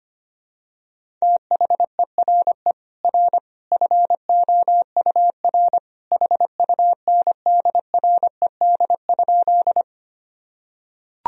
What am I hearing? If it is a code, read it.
Morse code, heard as a keyed tone, about 25 wpm, 700 Hz. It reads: THERE R FOUR HUNDRED?
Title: Sentences from Top 400 Words and Abbreviations 25wpm